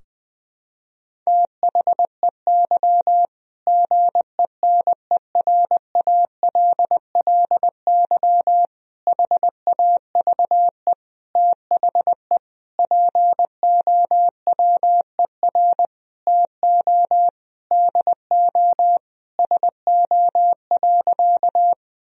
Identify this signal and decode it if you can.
Morse code, heard as a keyed tone, about 21 wpm, 700 Hz. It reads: THEY GENERALLY HAVE THE POWER TO DO SO.
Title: Sentences from Top 1000 Words - Review Part 4 - 20wpm